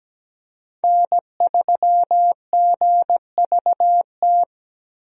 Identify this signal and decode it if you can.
Morse code, heard as a keyed tone, about 17 wpm, 700 Hz. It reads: N3GVT
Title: USA Call Signs - 5 Characters - 17wpm